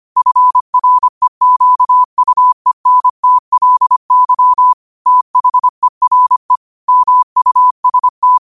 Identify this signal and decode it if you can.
Morse code, heard as a keyed tone, about 25 wpm, 1000 Hz. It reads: FREQUENTLY THERE MUST
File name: qotdfiles / 25wpm